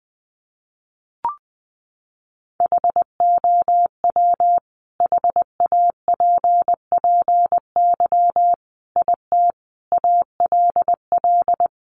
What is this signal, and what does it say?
Morse code, heard as a keyed tone, about 19 wpm, 700 Hz. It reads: HOW HAPPY IT ALL
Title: Sentences from Top 1000 Words 20wpm